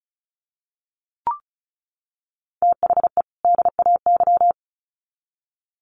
Morse code, heard as a keyed tone, about 34 wpm, 700 Hz. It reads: THE DAY
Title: Sets of 2 Words 35wpm